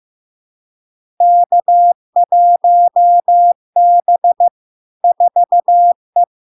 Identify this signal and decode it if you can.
Morse code, heard as a keyed tone, about 15 wpm, 700 Hz. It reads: K1B 4E
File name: LARK-FD-2022-Log-in-CW-15wpm-700-Hz